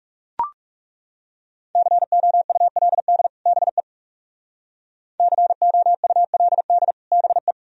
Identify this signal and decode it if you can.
Morse code, heard as a keyed tone, about 37 wpm, 700 Hz. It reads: COULD BE COULD BE
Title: Speed-Racing - Sets of 2 Words 30wpm